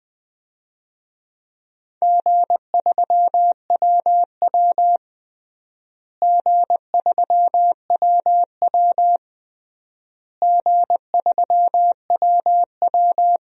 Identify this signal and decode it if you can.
Morse code, heard as a keyed tone, about 20 wpm, 700 Hz. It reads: G3WW G3WW G3WW
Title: DXCC-100 Call Signs - 4 Characters - Repeated 3x - 20wpm